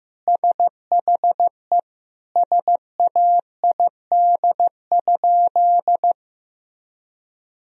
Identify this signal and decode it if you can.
Morse code, heard as a keyed tone, about 15 wpm, 700 Hz. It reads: SHE SAID?
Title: Sentences from Top 1000 Words - Review Part 1 - 15wpm